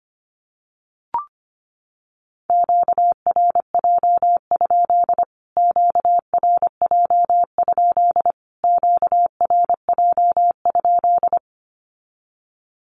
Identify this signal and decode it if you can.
Morse code, heard as a keyed tone, about 25 wpm, 700 Hz. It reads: QRJ? QRJ? QRJ?
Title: Q-Codes Repeated 3x 25wpm